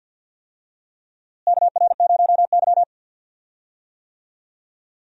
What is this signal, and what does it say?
Morse code, heard as a keyed tone, about 47 wpm, 700 Hz. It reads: KR0Y
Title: USA Call Signs - 4 Characters - Repeated 2x - 50wpm